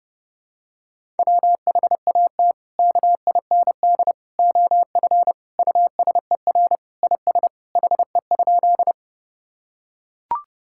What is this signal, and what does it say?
Morse code, heard as a keyed tone, about 30 wpm, 700 Hz. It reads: WHAT KIND OF USER IS HE?
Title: Sentences from Top 1000 Words - Review Part 5 - 30wpm